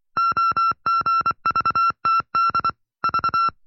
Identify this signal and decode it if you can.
Morse code, heard as a keyed tone, about 24 wpm, 1350 Hz. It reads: OGVTD V
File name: FreeFiles / CW-practice / 20WPM-side1